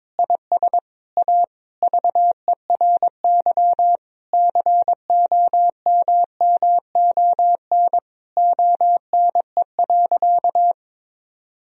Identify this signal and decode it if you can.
Morse code, heard as a keyed tone, about 22 wpm, 700 Hz. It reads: IS A VERY COMMON ONE.
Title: Sentences from Top 1000 Words - Review Part 4 - 22wpm